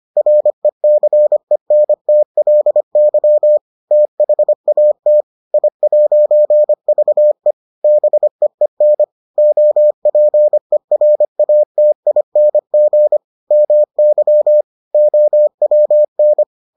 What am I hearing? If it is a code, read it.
Morse code, heard as a keyed tone, about 25 wpm, 600 Hz. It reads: RECENTLY THAT I'VE BEEN OPERATING MY OWN